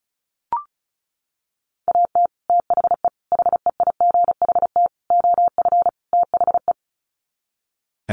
Morse code, heard as a keyed tone, about 34 wpm, 700 Hz. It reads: AT THE HEIGHT OF THE
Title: Sets of 5 Words 35wpm